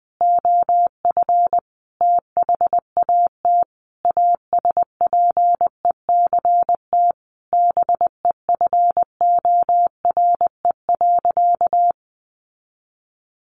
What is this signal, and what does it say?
Morse code, heard as a keyed tone, about 19 wpm, 700 Hz. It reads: OF THAT ASPECT BEFORE.
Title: Sentences from Top 1300 Words 20wpm